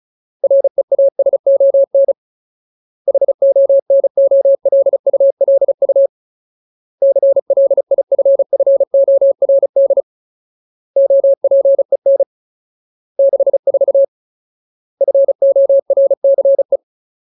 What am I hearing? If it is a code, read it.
Morse code, heard as a keyed tone, about 36 wpm, 550 Hz. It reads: REASON HONOLULU CLIFFORD OPEN 64 FORCE
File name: Ultimate Word List / 35WPM Ultimate